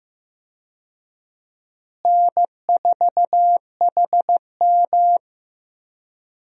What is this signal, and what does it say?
Morse code, heard as a keyed tone, about 16 wpm, 700 Hz. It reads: N4HM
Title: USA Call Signs - 4 Characters - Repeated 3x - 15wpm